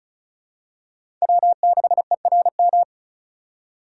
Morse code, heard as a keyed tone, about 36 wpm, 700 Hz. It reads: W6ERM
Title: USA Call Signs - 5 Characters - 3x Character Spacing - 35wpm